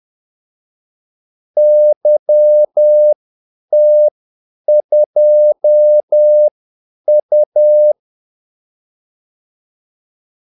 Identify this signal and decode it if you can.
Morse code, heard as a keyed tone, about 10 wpm, spaced as for 8 wpm, 600 Hz. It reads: YT2U
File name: Callsigns-10-8wpm